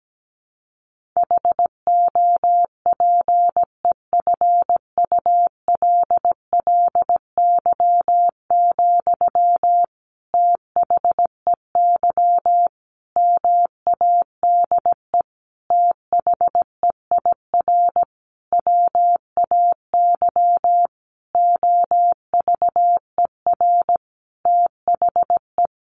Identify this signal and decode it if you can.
Morse code, heard as a keyed tone, about 17 wpm, 700 Hz. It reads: HOPEFULLY, THEY MADE THEIR WAY OVER THE
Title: Sentences from Top 1500 Words 17wpm